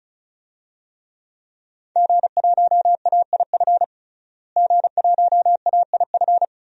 Morse code, heard as a keyed tone, about 34 wpm, 700 Hz. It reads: G1AIF G1AIF
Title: DXCC-100 Call Signs - 5 Characters - Repeated 3x - 35wpm